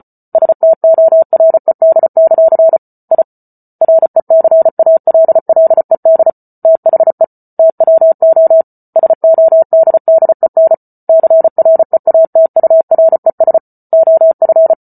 Morse code, heard as a keyed tone, about 34 wpm, 650 Hz. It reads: STORED; I RECALLED THE TWO SODDEN CREATURES OF